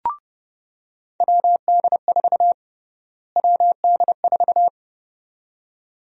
Morse code, heard as a keyed tone, about 31 wpm, 700 Hz. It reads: WD4 WD4
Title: USA Call Sign Prefixes - Repeated 2x - 30wpm